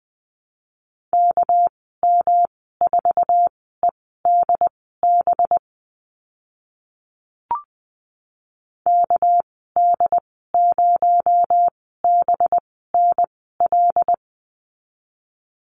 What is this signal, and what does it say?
Morse code, heard as a keyed tone, about 20 wpm, spaced as for 14 wpm, 700 Hz. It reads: KM4EDB KD0BNL
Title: 245 Lesson 39g - Letter z - Call Signs 20wpm